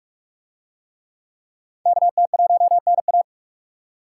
Morse code, heard as a keyed tone, about 42 wpm, 700 Hz. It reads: KT1NA